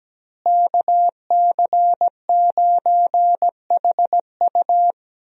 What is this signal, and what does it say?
Morse code, heard as a keyed tone, about 17 wpm, 700 Hz. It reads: KC9HU
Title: USA Call Signs - 5 Characters - Repeated 3x - 17wpm